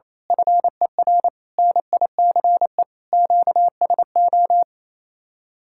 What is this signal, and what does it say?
Morse code, heard as a keyed tone, about 26 wpm, 700 Hz. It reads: FER NICE QSO
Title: Speed-Racing Common QSO Phrases - 28wpm